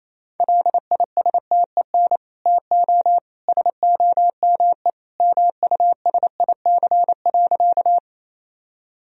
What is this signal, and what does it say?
Morse code, heard as a keyed tone, about 28 wpm, 700 Hz. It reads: LISTEN TO SOME MUSIC.
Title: Sentences from Top 1000 Words - Review Part 1 - 28wpm